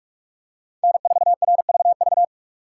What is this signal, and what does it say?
Morse code, heard as a keyed tone, about 45 wpm, 700 Hz. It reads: N4RVV